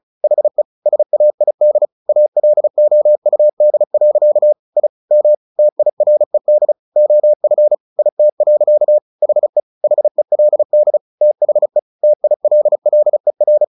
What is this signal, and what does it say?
Morse code, heard as a keyed tone, about 34 wpm, 600 Hz. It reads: HE SAID ALOUD. I M TIRED OF IT. HE HELD THE TILLER
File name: the old man and the sea 2 35wpm